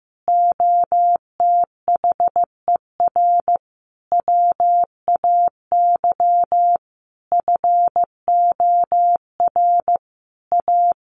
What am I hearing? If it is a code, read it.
Morse code, heard as a keyed tone, about 15 wpm, 700 Hz. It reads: OTHER WAY FOR A
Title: Sentences from Top 1300 Words 15wpm